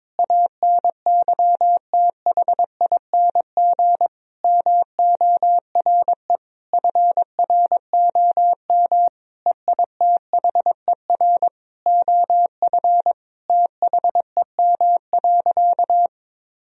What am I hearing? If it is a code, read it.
Morse code, heard as a keyed tone, about 22 wpm, 700 Hz. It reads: ANYTHING MORE FROM EITHER OF THEM.
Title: Sentences from Top 1000 Words - Review Part 4 - 22wpm